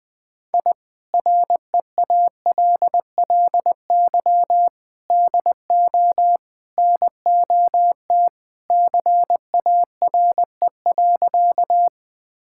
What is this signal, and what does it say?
Morse code, heard as a keyed tone, about 20 wpm, 700 Hz. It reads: I REALLY DO NOT CARE.
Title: Sentences with 4-Word Phrases - Spoken Beginning - 20wpm